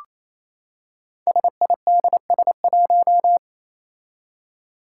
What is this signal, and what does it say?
Morse code, heard as a keyed tone, about 27 wpm, 700 Hz. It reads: SIDS1
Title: Five Challenging Letter-Number Combinations 28wpm